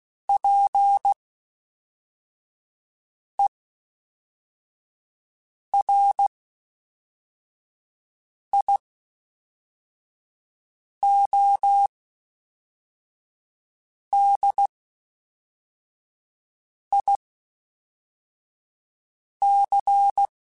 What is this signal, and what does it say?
Morse code, heard as a keyed tone, about 16 wpm, spaced as for 4 wpm, 800 Hz. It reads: PERIODIC
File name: (spaced) / FromSMs spaced mp3s / Classes@5